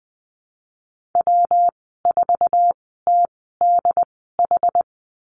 Morse code, heard as a keyed tone, about 20 wpm, spaced as for 15 wpm, 700 Hz. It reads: W4TDH
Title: 186 Lesson 30g - Letter b - Call Signs 20wpm